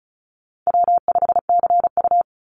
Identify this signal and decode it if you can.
Morse code, heard as a keyed tone, about 34 wpm, 700 Hz. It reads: W5CU
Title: Call Signs Repeated 3x - 35wpm